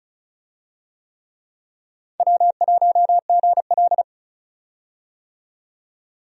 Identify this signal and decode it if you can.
Morse code, heard as a keyed tone, about 36 wpm, 700 Hz. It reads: W1GL
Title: USA Call Signs - 4 Characters - 35wpm